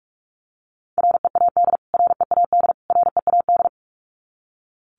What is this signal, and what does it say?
Morse code, heard as a keyed tone, about 44 wpm, 700 Hz. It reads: READ READ READ
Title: Top 500 Words Repeated 45wpm